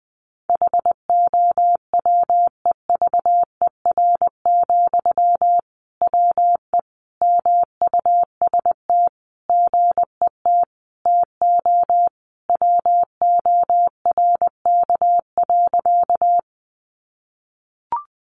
Morse code, heard as a keyed tone, about 20 wpm, 700 Hz. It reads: HOWEVER, WE MUST GET TO WORK.
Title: Sentences from Top 600 Words 20wpm